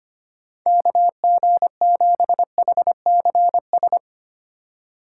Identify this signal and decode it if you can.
Morse code, heard as a keyed tone, about 30 wpm, 700 Hz. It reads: KG7HCS
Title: Speed Racing - USA 2x3 Call Signs 25wpm